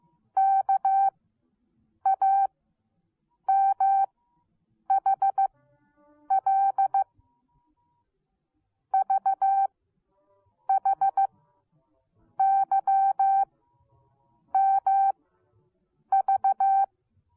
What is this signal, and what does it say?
Morse code, heard as a keyed tone, about 15 wpm, spaced as for 7 wpm, 780 Hz. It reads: KAMHL VHYMV